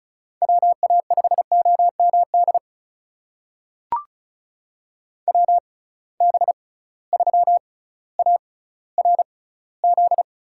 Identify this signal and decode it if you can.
Morse code, heard as a keyed tone, about 36 wpm, spaced as for 18 wpm, 700 Hz. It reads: WA5OMD WB3ARZ
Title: USA Call Signs - 6 Characters - 6x Character Spacing - 35wpm